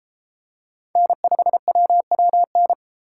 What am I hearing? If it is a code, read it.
Morse code, heard as a keyed tone, about 29 wpm, 700 Hz. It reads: N5WWN